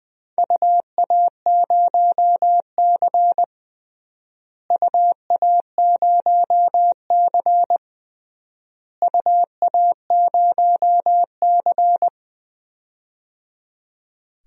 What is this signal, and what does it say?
Morse code, heard as a keyed tone, about 20 wpm, 700 Hz. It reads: UA0C UA0C UA0C
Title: DXCC-100 Call Signs - 4 Characters - Repeated 3x - 20wpm